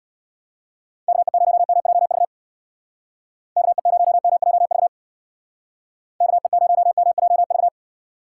Morse code, heard as a keyed tone, about 62 wpm, 700 Hz. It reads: KE0MJV KE0MJV KE0MJV
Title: Speed Racing - USA Call Signs 50wpm